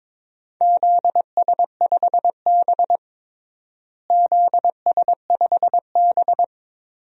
Morse code, heard as a keyed tone, about 22 wpm, 700 Hz. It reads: ZS5B ZS5B
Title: DXCC-100 Call Signs - 4 Characters - Repeated 2x - 22wpm